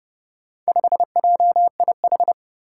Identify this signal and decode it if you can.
Morse code, heard as a keyed tone, about 30 wpm, 700 Hz. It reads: 5JIH
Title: Four Challenging Letter-Number Combinations 30wpm